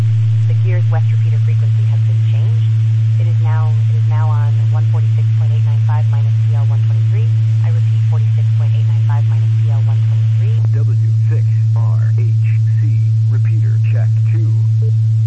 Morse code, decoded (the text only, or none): none